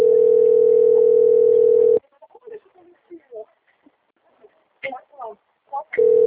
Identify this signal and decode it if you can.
Ringback tone (North America), 440 + 480 Hz, with speech.
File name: out- - - - - .38431